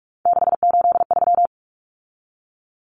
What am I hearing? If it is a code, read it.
Morse code, heard as a keyed tone, about 44 wpm, 700 Hz. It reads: 683
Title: Numbers - 3-Digit 45wpm